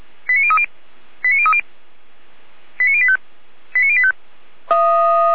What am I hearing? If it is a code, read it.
Five-tone selective call (ZVEI), heard as quick runs of five tones, tones of about 70 ms. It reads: 89030 89030 89085 89085